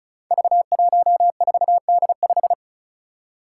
Morse code, heard as a keyed tone, about 36 wpm, 700 Hz. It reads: V14D5
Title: Five Challenging Letter-Number Combinations 35wpm